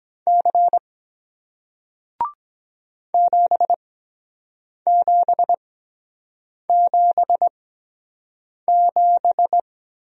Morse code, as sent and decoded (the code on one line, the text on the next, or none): -.-. / --... --... --... --...
C 7777